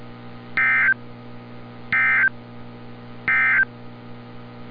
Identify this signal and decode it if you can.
Emergency Alert System (SAME) tones, heard as harsh buzzing bursts. It is heard from 0.5 to 3.7 s.
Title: NOAA Weather Radio - Altoona, PA